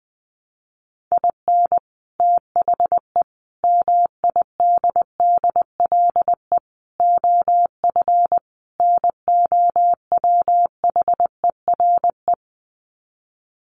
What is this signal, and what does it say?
Morse code, heard as a keyed tone, about 20 wpm, 700 Hz. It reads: IN THE MIDDLE OF NOWHERE
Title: Sets of 5 Words Encore 20wpm